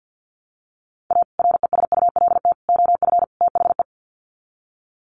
Morse code, heard as a keyed tone, about 50 wpm, 700 Hz. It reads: A RESULT OF THE
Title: Sets of 4 Words 50wpm